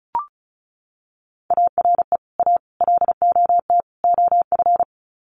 Morse code, heard as a keyed tone, about 34 wpm, 700 Hz. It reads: ARE A LOT OF
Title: Sets of 4 Words 35wpm